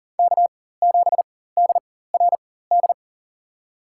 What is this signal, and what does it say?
Morse code, heard as a keyed tone, about 40 wpm, spaced as for 19 wpm, 700 Hz. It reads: K7DRD
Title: USA Call Signs - 5 Characters - 4x Character Spacing - 40wpm